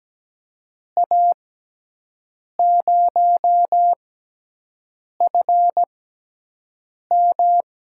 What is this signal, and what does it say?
Morse code, heard as a keyed tone, about 17 wpm, spaced as for 6 wpm, 700 Hz. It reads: A0FM
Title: USA Call Signs - 6 Characters - 6x Character Spacing - 17wpm